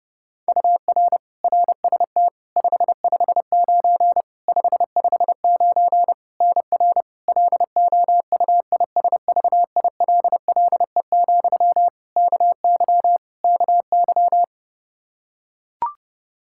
Morse code, heard as a keyed tone, about 30 wpm, 700 Hz. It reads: UR RST 559 559 NR LOUISVILLE, KY KY
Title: Practice QSO - Exchange 1 - Beginning - 30wpm